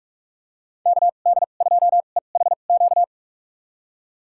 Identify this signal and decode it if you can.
Morse code, heard as a keyed tone, about 47 wpm, spaced as for 33 wpm, 700 Hz. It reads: KD2EHQ
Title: USA Call Signs - 2x Character Spacing - 45wpm